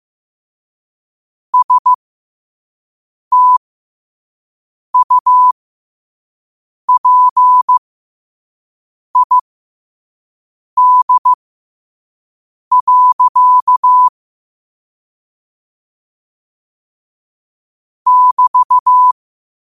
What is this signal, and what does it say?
Morse code, heard as a keyed tone, about 15 wpm, spaced as for 5 wpm, 1000 Hz. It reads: STUPID. <BT>
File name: qotdfiles / 5wpm